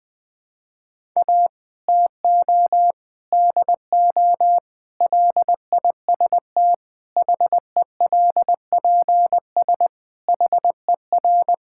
Morse code, heard as a keyed tone, about 21 wpm, 700 Hz. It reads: A TO DO LIST HELPS HER
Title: Difficult Sentences with Hints - 20wpm